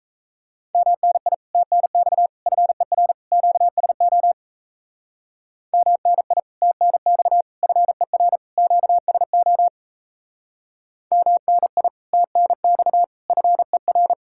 Morse code, heard as a keyed tone, about 36 wpm, 700 Hz. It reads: MNI TNX FER QSO MNI TNX FER QSO MNI TNX FER